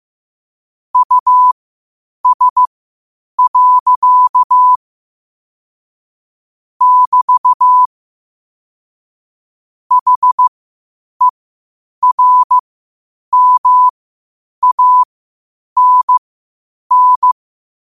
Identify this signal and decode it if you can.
Morse code, heard as a keyed tone, about 15 wpm, spaced as for 8 wpm, 1000 Hz. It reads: US. <BT> HERMANN
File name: qotdfiles / 8wpm